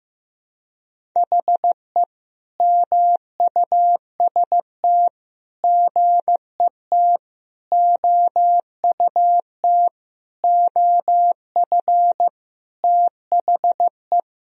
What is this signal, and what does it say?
Morse code, heard as a keyed tone, about 15 wpm, 700 Hz. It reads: HE MUST GET OUT OF THE
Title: Sentences from Top 1000 Words - Review Part 1 - 15wpm